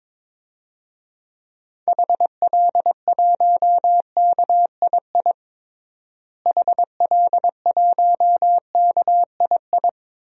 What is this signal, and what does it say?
Morse code, heard as a keyed tone, about 22 wpm, 700 Hz. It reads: HL1KII HL1KII
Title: DXCC-100 Call Signs - 6 Characters - Repeated 2x - 22wpm